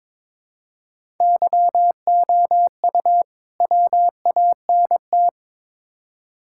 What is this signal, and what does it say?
Morse code, heard as a keyed tone, about 22 wpm, 700 Hz. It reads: YOU WANT
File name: Build A Phrase - 2-Word Phrases - 22wpm